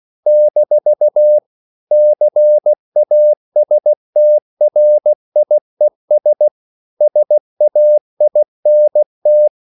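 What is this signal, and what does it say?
Morse code, heard as a keyed tone, about 16 wpm, 600 Hz. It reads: - CASTRIES SAINT
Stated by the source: CW practice of national capitals at 16 words per minute, from ON6ZQ.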